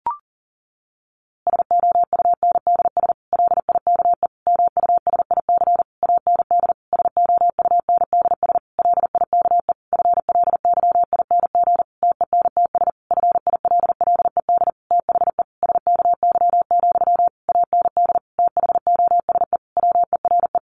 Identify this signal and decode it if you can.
Morse code, heard as a keyed tone, about 40 wpm, 700 Hz. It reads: SOUNDS LIKE MUSIC AND SOUNDS LIKE FLYING TENTS FILLED THE SKY, AND THOSE WERE
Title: A Sound Of Thunder 40wpm